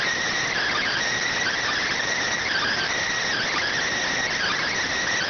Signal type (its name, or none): Emergency Alert System (SAME) tones